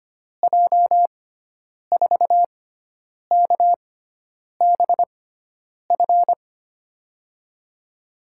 Morse code, heard as a keyed tone, about 25 wpm, spaced as for 9 wpm, 700 Hz. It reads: J4KBF
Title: USA Call Signs - 6 Characters - 6x Character Spacing - 25wpm